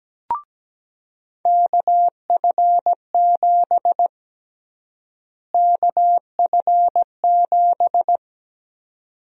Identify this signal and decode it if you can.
Morse code, heard as a keyed tone, about 17 wpm, 700 Hz. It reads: KF7 KF7
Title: USA Call Sign Prefixes - Repeated 2x - 17wpm